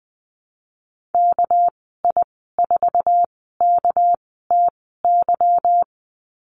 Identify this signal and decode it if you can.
Morse code, heard as a keyed tone, about 20 wpm, spaced as for 14 wpm, 700 Hz. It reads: KI4KTY
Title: 245 Lesson 39g - Letter z - Call Signs 20wpm